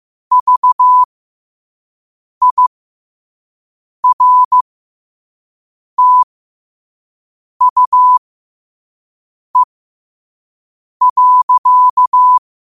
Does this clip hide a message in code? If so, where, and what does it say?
Yes: Morse code from 0.3 to 12.4 s, heard as a keyed tone, about 15 wpm, spaced as for 5 wpm, 1000 Hz. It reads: VIRTUE.